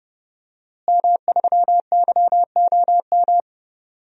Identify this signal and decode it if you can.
Morse code, heard as a keyed tone, about 30 wpm, 700 Hz. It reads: M3YOM
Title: DXCC-100 Call Signs - 5 Characters - Repeated 3x - 30wpm